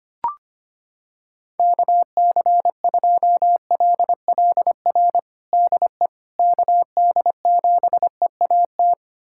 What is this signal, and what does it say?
Morse code, heard as a keyed tone, about 25 wpm, 700 Hz. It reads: KC2LLR DE KD7EAT
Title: USA Callsign DE Callsign - 6 Characters - 25wpm